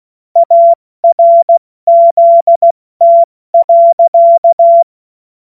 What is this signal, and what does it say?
Morse code, heard as a keyed tone, about 16 wpm, 680 Hz. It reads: ARZT.
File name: Witz-09-September-2025-70-F80